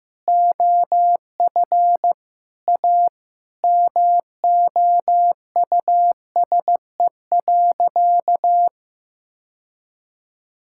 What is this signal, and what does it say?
Morse code, heard as a keyed tone, about 15 wpm, 700 Hz. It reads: OF A MOUSE.
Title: Sentences from Top 1000 Words - Review Part 5 - 15wpm